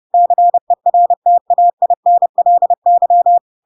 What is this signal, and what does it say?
Morse code, heard as a keyed tone, about 30 wpm, 700 Hz. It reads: CERTAINLY
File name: Pinocchio / Pinocchio ch5 30 30 ik1hge